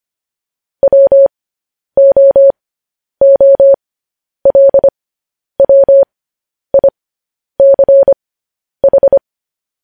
Morse code, heard as a keyed tone, about 25 wpm, spaced as for 10 wpm, 550 Hz. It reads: WOOLWICH
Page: news-headlines-in-morse-code-at-10-wpm